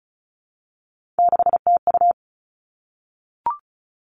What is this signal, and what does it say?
Morse code, heard as a keyed tone, about 36 wpm, 700 Hz. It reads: 6TU
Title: Three Challenging Letter-Number Combinations 35wpm